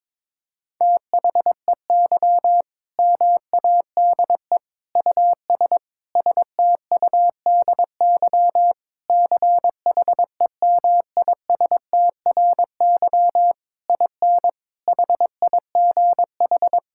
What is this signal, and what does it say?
Morse code, heard as a keyed tone, about 22 wpm, 700 Hz. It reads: THEY MADE US STUDY CHEMISTRY IN HIGH
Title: Sentences from Top 1000 Words - Review Part 6 - 22wpm